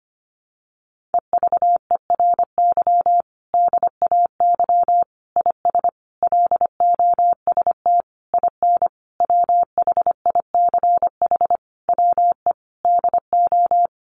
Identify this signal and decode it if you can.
Morse code, heard as a keyed tone, about 25 wpm, 700 Hz. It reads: EVERY DAY IS LOST IN WHICH WE DO
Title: Sentences from Top 1400 Words 25wpm